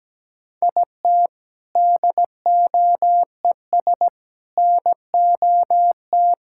Morse code, heard as a keyed tone, about 17 wpm, 700 Hz. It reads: IT DOES NOT
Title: Sentences from Top 1000 Words - Review Part 6 - 17wpm